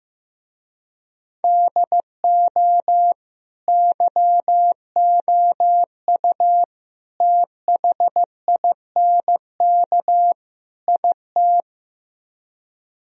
Morse code, heard as a keyed tone, about 15 wpm, 700 Hz. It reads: DO YOU THINK IT
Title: Speed-Racing - Sets of 4 Words 15wpm